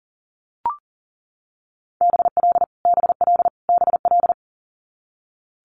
Morse code, heard as a keyed tone, about 40 wpm, 700 Hz. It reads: BL BL BL
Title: Multi-letter Phonemes Repeated 3x 40wpm